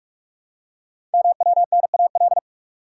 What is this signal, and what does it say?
Morse code, heard as a keyed tone, about 44 wpm, 700 Hz. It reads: MWNAL
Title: Code Groups - 5 Letters - 45wpm